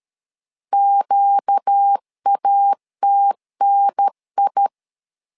Morse code, heard as a keyed tone, about 13 wpm, 800 Hz. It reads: QATNI
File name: LEZIONI / 5 LBVC / 5 LB 12wpm 1906